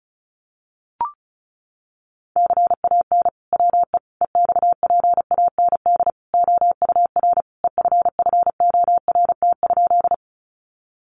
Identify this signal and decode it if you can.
Morse code, heard as a keyed tone, about 34 wpm, 700 Hz. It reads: CAN WE EXPAND OUR EFFORT?
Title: Sentences from Top 1400 Words 35wpm